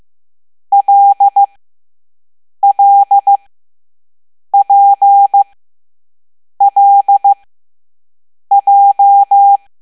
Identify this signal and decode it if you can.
Morse code, heard as a keyed tone, about 15 wpm, spaced as for 6 wpm, 800 Hz. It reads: LLPLJ